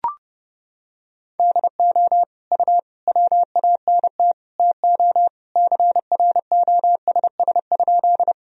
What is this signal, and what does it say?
Morse code, heard as a keyed tone, about 32 wpm, 700 Hz. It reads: DO U WANT TO CROSS?
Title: Sentences from Top 300 Words and Abbreviations 30wpm